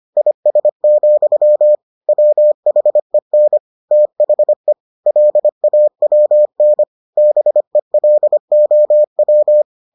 Morse code, heard as a keyed tone, about 25 wpm, 600 Hz. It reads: IS, WHEN THE LAWN BELOW